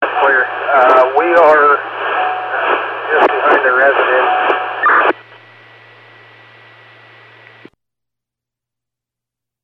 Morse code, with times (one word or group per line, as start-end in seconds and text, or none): none